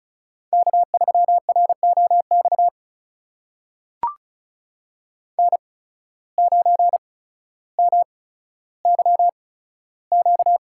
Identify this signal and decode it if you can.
Morse code, heard as a keyed tone, about 36 wpm, spaced as for 15 wpm, 700 Hz. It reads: K3ROX N9MYQ
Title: USA Call Signs - 5 Characters - 8x Character Spacing - 35wpm